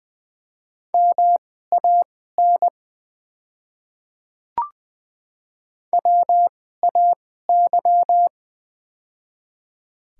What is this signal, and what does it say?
Morse code, heard as a keyed tone, about 20 wpm, spaced as for 15 wpm, 700 Hz. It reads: MAN WAY
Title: 2x Character Spacing - Top 300 Words - 20wpm